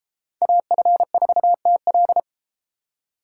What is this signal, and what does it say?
Morse code, heard as a keyed tone, about 31 wpm, 700 Hz. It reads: AF4TL